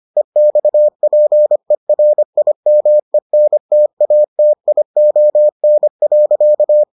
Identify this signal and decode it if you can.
Morse code, heard as a keyed tone, about 25 wpm, 600 Hz. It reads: EXPERIMENTATION.